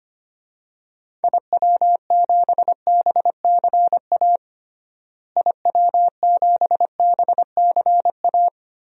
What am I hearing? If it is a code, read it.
Morse code, heard as a keyed tone, about 25 wpm, 700 Hz. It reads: IW7BCA IW7BCA
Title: DXCC-100 Call Signs - 6 Characters - Repeated 2x - 25wpm